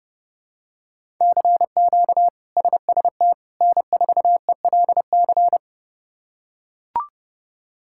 Morse code, heard as a keyed tone, about 30 wpm, 700 Hz. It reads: CQ SST N4ELC
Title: K1USN SST QSO Practice - 30wpm